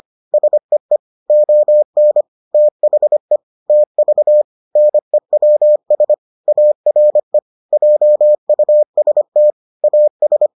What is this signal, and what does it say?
Morse code, heard as a keyed tone, about 25 wpm, 600 Hz. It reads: SEE ON THE TV NEWS ARE JUST AS